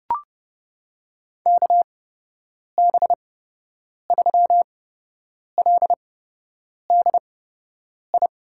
Morse code, heard as a keyed tone, about 30 wpm, spaced as for 8 wpm, 700 Hz. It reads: KB3LDI
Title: USA Call Signs - 8x Character Spacing - 30wpm